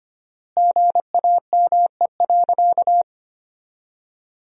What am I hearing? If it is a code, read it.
Morse code, heard as a keyed tone, about 25 wpm, 700 Hz. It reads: GAME.
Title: 4x Word-Spacing - Sentences from Top 500 Words - 25wpm